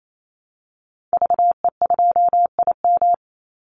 Morse code, heard as a keyed tone, about 28 wpm, 700 Hz. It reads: VE2IM
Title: Call Signs Repeated 2x - 28wpm